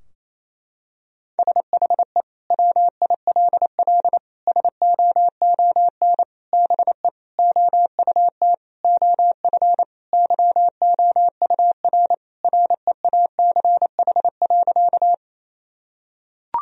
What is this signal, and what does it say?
Morse code, heard as a keyed tone, about 27 wpm, 700 Hz. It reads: SHE WILL SOON BE OUT OF YOUR REACH.
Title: Sentences from Top 1000 Words - Review Part 2 - 28wpm